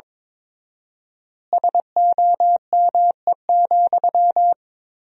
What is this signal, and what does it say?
Morse code, heard as a keyed tone, about 22 wpm, 700 Hz. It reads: SOME,
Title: 4x Word-Spacing - Sentences from Top 200 Words - 22wpm